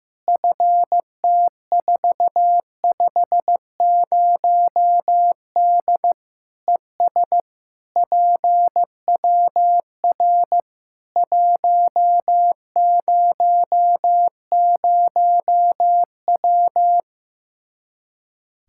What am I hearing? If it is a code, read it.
Morse code, heard as a keyed tone, about 15 wpm, 700 Hz. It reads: FT450D ES PWR 100W
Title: K1USN SST QSO Practice 15wpm